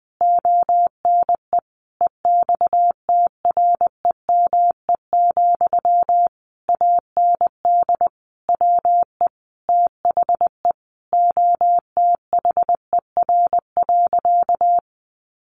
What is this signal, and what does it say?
Morse code, heard as a keyed tone, about 20 wpm, 700 Hz. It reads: ONE EXTREME, AND WE THE OTHER.
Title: Sentences from Top 1300 Words 20wpm